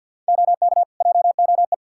Morse code, heard as a keyed tone, about 51 wpm, 700 Hz. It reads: OK JOE
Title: Practice QSO - Exchange 3 - More Info - 50wpm